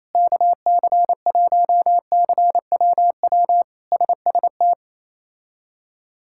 Morse code, heard as a keyed tone, about 27 wpm, 700 Hz. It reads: KC1CWW SST
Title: K1USN SST QSO Practice - 28wpm